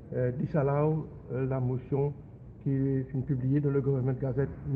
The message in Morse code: none